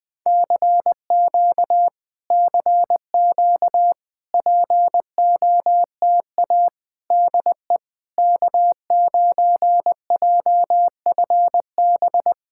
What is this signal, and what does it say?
Morse code, heard as a keyed tone, about 20 wpm, 700 Hz. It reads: CQ CQ POTA DE K9JFB
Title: POTA QSO Practice - 20wpm